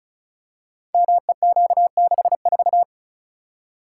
Morse code, heard as a keyed tone, about 36 wpm, 700 Hz. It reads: MEQ64
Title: Code Groups - 5 Letter:Numbers - 35wpm